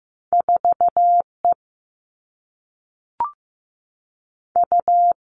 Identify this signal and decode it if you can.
Morse code, heard as a keyed tone, about 15 wpm, 700 Hz. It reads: 4 E U
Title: Two Challenging Letter-Number Combinations 15wpm